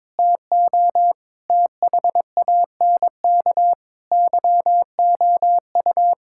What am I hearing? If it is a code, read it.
Morse code, heard as a keyed tone, about 22 wpm, 700 Hz. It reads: TO THANK YOU
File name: Sentences with 9-Word Phrases - 22wpm